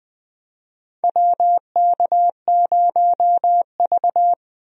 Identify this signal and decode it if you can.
Morse code, heard as a keyed tone, about 20 wpm, 700 Hz. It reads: WK0V
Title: USA Call Signs - 4 Characters - 20wpm